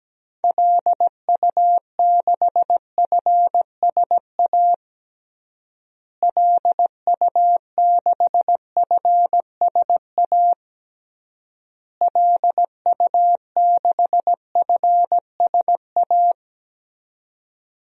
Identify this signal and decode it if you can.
Morse code, heard as a keyed tone, about 17 wpm, 700 Hz. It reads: LU6FSA LU6FSA LU6FSA
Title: DXCC-100 Call Signs - 6 Characters - Repeated 3x - 17wpm